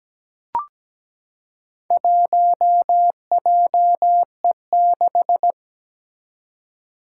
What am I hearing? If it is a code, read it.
Morse code, heard as a keyed tone, about 17 wpm, 700 Hz. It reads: 1JE6
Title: Four Challenging Letter-Number Combinations 17wpm